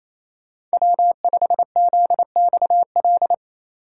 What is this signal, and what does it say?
Morse code, heard as a keyed tone, about 35 wpm, 700 Hz. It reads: W5ZXL
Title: Speed Racing - USA 1x3 Call Signs 28wpm